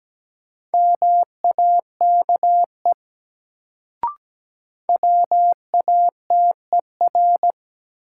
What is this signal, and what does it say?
Morse code, heard as a keyed tone, about 17 wpm, 700 Hz. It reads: MAKE WATER
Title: Sentences from Top 100 Words - New Word Drill - 17wpm